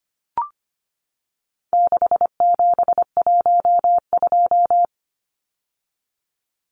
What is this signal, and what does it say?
Morse code, heard as a keyed tone, about 25 wpm, 700 Hz. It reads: 6712
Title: Numbers - 4-Digit 25wpm